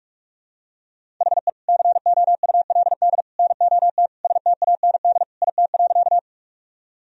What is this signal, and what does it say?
Morse code, heard as a keyed tone, about 41 wpm, 700 Hz. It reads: HE COULD NOT STAND IT.
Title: Sentences from Top 1000 Words - Review Part 1 - 45wpm